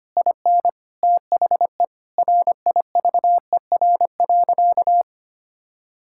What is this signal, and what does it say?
Morse code, heard as a keyed tone, about 25 wpm, 700 Hz. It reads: IN THE RIVER.
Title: Sentences from Top 1600 Words 25wpm